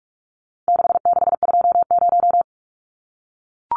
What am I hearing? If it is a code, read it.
Morse code, heard as a keyed tone, about 44 wpm, 700 Hz. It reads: 6620
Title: Numbers - 4-Digit 45wpm